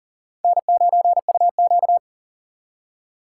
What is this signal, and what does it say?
Morse code, heard as a keyed tone, about 54 wpm, spaced as for 47 wpm, 700 Hz. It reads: N9UQ
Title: Speed Racing - USA 1x2 Call Signs 40wpm